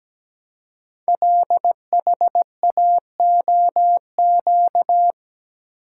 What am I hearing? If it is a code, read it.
Morse code, heard as a keyed tone, about 17 wpm, 700 Hz. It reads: LHAOQ
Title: Code Groups - 5 Letters - 17wpm